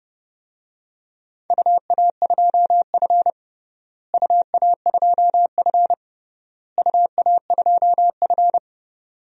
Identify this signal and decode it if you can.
Morse code, heard as a keyed tone, about 30 wpm, 700 Hz. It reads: UA2F UA2F UA2F
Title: DXCC-100 Call Signs - 4 Characters - Repeated 3x - 30wpm